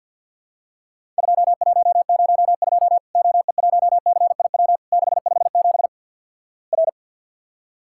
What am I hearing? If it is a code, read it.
Morse code, heard as a keyed tone, about 50 wpm, 700 Hz. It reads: 2102 OE1CIW 657 R
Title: Worked All Europe DX Contest - QTC Exchange - 50wpm